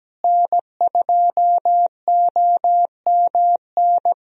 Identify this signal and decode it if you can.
Morse code, heard as a keyed tone, about 17 wpm, 700 Hz. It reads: N2OMN
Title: USA Call Signs - 5 Characters - Repeated 3x - 17wpm